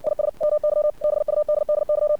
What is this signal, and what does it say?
Morse code, heard as a keyed tone, about 60 wpm, 600 Hz. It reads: IS MY BUDDY
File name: archived / CRSnet